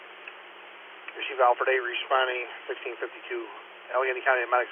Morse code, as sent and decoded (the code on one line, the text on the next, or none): none